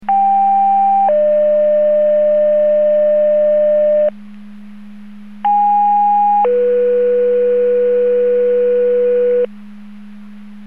Two-tone page, heard as a one-second tone, then a three-second tone, about 790 then 600 Hz, about 820 then 490 Hz. It is heard from 0.1 to 9.5 s.
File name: Portage 154.385 Paging